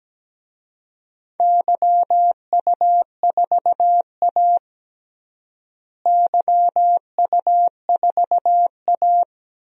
Morse code, heard as a keyed tone, about 17 wpm, 700 Hz. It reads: YU4A YU4A
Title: DXCC-100 Call Signs - 4 Characters - Repeated 3x - 17wpm